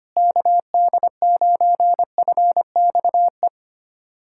Morse code, heard as a keyed tone, about 25 wpm, 700 Hz. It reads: KD9FXE
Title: SOTA QSO Practice 25wpm